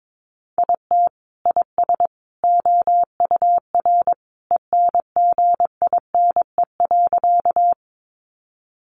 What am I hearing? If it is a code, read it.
Morse code, heard as a keyed tone, about 22 wpm, 700 Hz. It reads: IT IS OUR ENGINE.
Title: Sentences from Top 1500 Words 22wpm